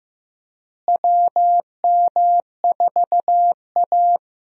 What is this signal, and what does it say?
Morse code, heard as a keyed tone, about 15 wpm, 700 Hz. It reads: WM4A
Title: USA Call Signs - 4 Characters - Repeated 3x - 15wpm